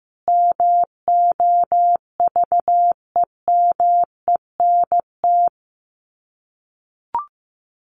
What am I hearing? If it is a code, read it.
Morse code, heard as a keyed tone, about 15 wpm, 700 Hz. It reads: MOVEMENT
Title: Sets of 5 Words Encore2 15wpm